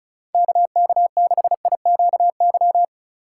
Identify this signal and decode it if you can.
Morse code, heard as a keyed tone, about 36 wpm, 700 Hz. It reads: KK6IQY
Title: USA Call Signs - 6 Characters - 3x Character Spacing - 35wpm